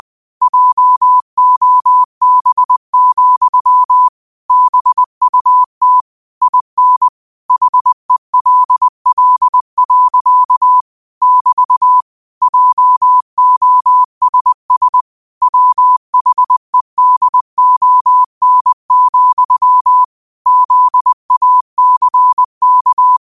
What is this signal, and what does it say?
Morse code, heard as a keyed tone, about 20 wpm, 1000 Hz. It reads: JOB, BUT IN HELL. <BT> JOSS WHEDON, ZACK
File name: qotdfiles / 20wpm